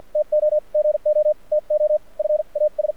Morse code, heard as a keyed tone, about 48 wpm, spaced as for 40 wpm, 600 Hz, with radio noise. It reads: TO GO TO FAS
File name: archived / CRSnet